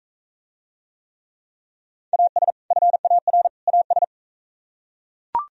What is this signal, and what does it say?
Morse code, heard as a keyed tone, about 45 wpm, 700 Hz. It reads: AS FAR AS